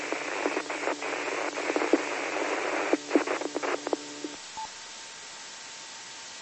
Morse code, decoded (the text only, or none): none